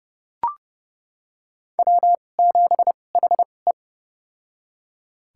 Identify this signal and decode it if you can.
Morse code, heard as a keyed tone, about 30 wpm, spaced as for 22 wpm, 700 Hz. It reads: W7HE
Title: USA Call Signs - 4 Characters - 2x Character Spacing - 30wpm